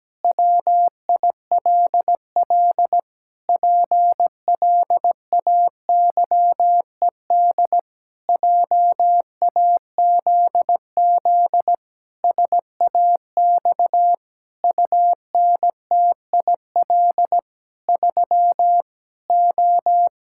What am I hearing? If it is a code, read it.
Morse code, heard as a keyed tone, about 17 wpm, 700 Hz. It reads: WILL PLAYED JAZZ SAX UNTIL 3 O
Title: Pangram Challenge v2 - 17wpm